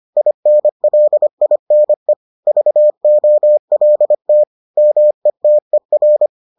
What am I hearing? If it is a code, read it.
Morse code, heard as a keyed tone, about 25 wpm, 600 Hz. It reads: INLINE VOLT METER